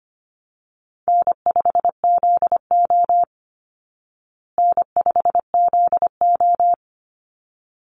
Morse code, heard as a keyed tone, about 25 wpm, 700 Hz. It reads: N5ZO N5ZO
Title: Easy Call Signs Repeated 2x - 25wpm